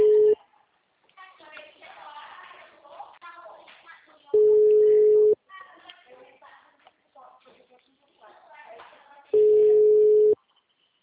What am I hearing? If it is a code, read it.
Ringback tone (Europe, 425 Hz), with speech.
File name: out- -2006- - - .157